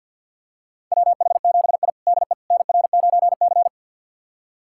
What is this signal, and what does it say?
Morse code, heard as a keyed tone, about 48 wpm, 700 Hz. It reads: WH7I DE NR9C
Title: USA Callsign DE Callsign - 4 Characters - 50wpm